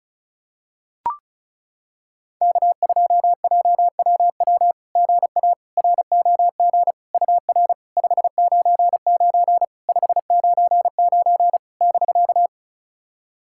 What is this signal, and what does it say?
Morse code, heard as a keyed tone, about 36 wpm, 690 Hz. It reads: K2JWW GA ROG UR 599 599 <BK>
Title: SOTA QSO Practice 35wpm